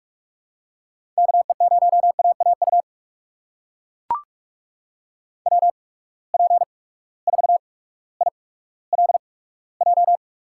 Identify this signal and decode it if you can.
Morse code, heard as a keyed tone, about 42 wpm, spaced as for 19 wpm, 700 Hz. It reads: KE0AAU WP4ILJ
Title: USA Call Signs - 6 Characters - 8x Character Spacing - 45wpm